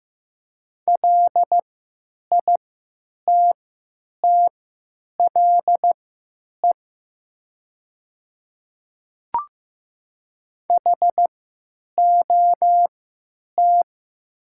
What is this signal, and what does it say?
Morse code, heard as a keyed tone, about 15 wpm, spaced as for 9 wpm, 700 Hz. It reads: LITTLE HOT
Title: 3x Character Spacing - Top 500 Words - 15wpm